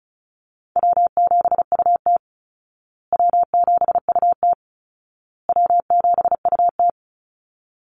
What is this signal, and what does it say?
Morse code, heard as a keyed tone, about 34 wpm, 700 Hz. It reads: W7UT W7UT W7UT
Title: Easy Call Signs Repeated 3x - 35wpm